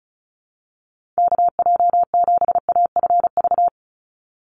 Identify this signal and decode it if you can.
Morse code, heard as a keyed tone, about 34 wpm, 700 Hz. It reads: KJ7AFV
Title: USA Call Signs 35wpm